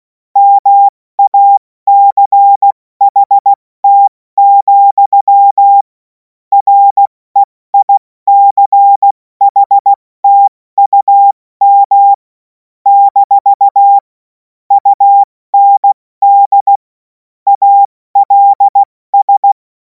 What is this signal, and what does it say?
Morse code, heard as a keyed tone, about 16 wpm, 800 Hz. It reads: MACHT, REICHTUM - UND ALS
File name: Morsekurs / Witze / Witz 26. Mai 2026 70 F80